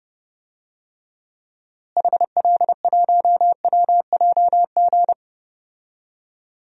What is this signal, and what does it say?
Morse code, heard as a keyed tone, about 30 wpm, 700 Hz. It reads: HL1WJG
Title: DXCC-100 Call Signs - 6 Characters - Repeated 2x - 30wpm